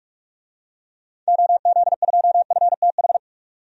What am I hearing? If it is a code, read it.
Morse code, heard as a keyed tone, about 42 wpm, 700 Hz. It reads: OZ2FTH
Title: DXCC-100 Call Signs - 6 Characters - 45wpm